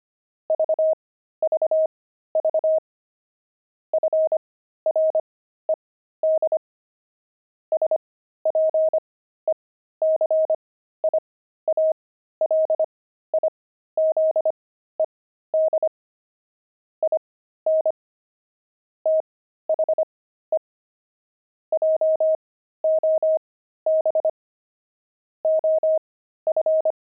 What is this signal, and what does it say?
Morse code, heard as a keyed tone, about 25 wpm, spaced as for 13 wpm, 630 Hz. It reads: VVV FRED SPECIALIZED IN THE JOB OF
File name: pangram-fs-13-cs-25